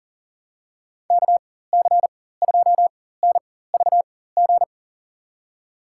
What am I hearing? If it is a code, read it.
Morse code, heard as a keyed tone, about 40 wpm, spaced as for 19 wpm, 700 Hz. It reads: KC2NVG
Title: USA Call Signs - 6 Characters - 4x Character Spacing - 40wpm